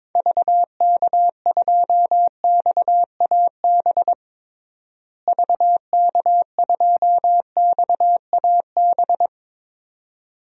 Morse code, heard as a keyed tone, about 22 wpm, 700 Hz. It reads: VK2XAB VK2XAB
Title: DXCC-100 Call Signs - 6 Characters - Repeated 2x - 22wpm